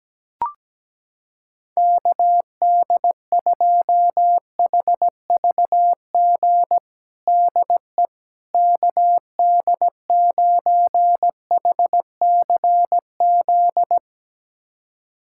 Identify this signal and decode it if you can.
Morse code, heard as a keyed tone, about 17 wpm, 700 Hz. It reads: KD2HVG DE KD9HCZ
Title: USA Callsign DE Callsign - 6 Characters - 17wpm